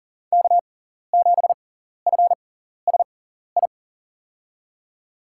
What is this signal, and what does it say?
Morse code, heard as a keyed tone, about 40 wpm, spaced as for 14 wpm, 700 Hz. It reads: K7FSI
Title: USA Call Signs - 6x Character Spacing - 40wpm